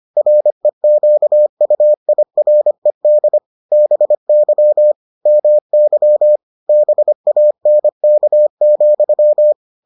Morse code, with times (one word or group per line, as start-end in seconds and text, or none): 0.2-3.4 REQUIRED
3.7-4.9 BY
5.3-6.4 MY
6.7-9.5 BANK,